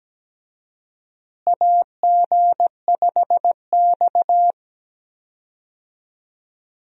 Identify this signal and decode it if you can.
Morse code, heard as a keyed tone, about 17 wpm, 700 Hz. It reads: AG5X
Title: USA Call Signs - 4 Characters - Repeated 3x - 17wpm